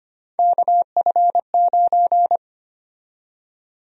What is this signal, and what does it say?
Morse code, heard as a keyed tone, about 25 wpm, 700 Hz. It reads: KF9
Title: USA Call Sign Prefixes - 25wpm